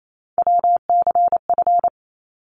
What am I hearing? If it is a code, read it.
Morse code, heard as a keyed tone, about 28 wpm, 700 Hz. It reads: WCF
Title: ARRL Sections 28wpm